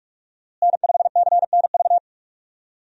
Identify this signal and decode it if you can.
Morse code, heard as a keyed tone, about 47 wpm, 700 Hz. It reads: N5CNV
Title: USA Call Signs - 5 Characters - Repeated 3x - 45wpm